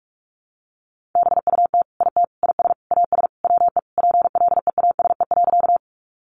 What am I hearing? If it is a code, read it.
Morse code, heard as a keyed tone, about 44 wpm, 700 Hz. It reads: BUT IT IS AS WE PLEASE.
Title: Sentences from Top 1500 Words 45wpm